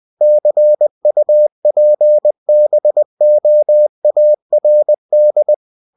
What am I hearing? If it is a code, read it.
Morse code, heard as a keyed tone, about 20 wpm, 600 Hz. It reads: CUPBOARD
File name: cw-books / winnie-the-pooh / winnie-20wpm-page-0030